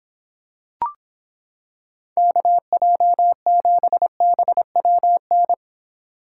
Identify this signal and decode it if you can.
Morse code, heard as a keyed tone, about 26 wpm, 700 Hz. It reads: KJ7BWN